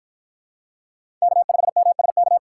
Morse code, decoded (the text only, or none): K5MSK